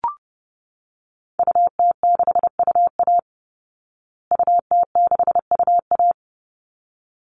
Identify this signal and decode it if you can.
Morse code, heard as a keyed tone, about 30 wpm, 700 Hz. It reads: UT6UA UT6UA
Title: Call Signs Repeated 3x - 30wpm